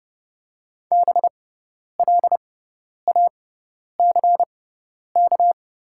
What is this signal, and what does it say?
Morse code, heard as a keyed tone, about 30 wpm, spaced as for 10 wpm, 700 Hz. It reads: BLACK
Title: 6x Character Spacing - Top 500 Words - 30wpm